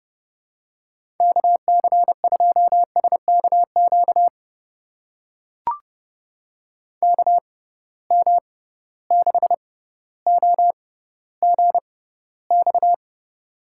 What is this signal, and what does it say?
Morse code, heard as a keyed tone, about 32 wpm, spaced as for 16 wpm, 700 Hz. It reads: KC2SKQ KM6OGX
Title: USA Call Signs - 6 Characters - 6x Character Spacing - 30wpm